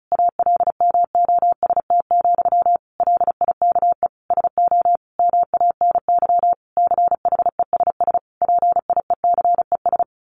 Morse code, heard as a keyed tone, about 34 wpm, 700 Hz. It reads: ALMOST, LIKE SO MANY CHESS PIECES
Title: A Sound Of Thunder 35wpm